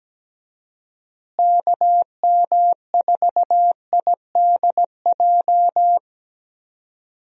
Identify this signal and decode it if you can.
Morse code, heard as a keyed tone, about 17 wpm, 700 Hz. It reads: KM4IDJ
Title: USA Call Signs - 6 Characters - Repeated 3x - 17wpm